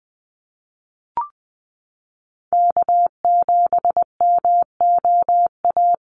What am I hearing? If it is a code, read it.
Morse code, heard as a keyed tone, about 20 wpm, 700 Hz. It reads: K7MOA
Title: Call Signs - 20wpm